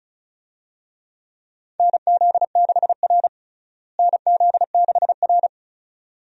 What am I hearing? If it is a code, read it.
Morse code, heard as a keyed tone, about 36 wpm, 700 Hz. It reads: NZ6R NZ6R
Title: USA Call Signs - 4 Characters - Repeated 2x - 35wpm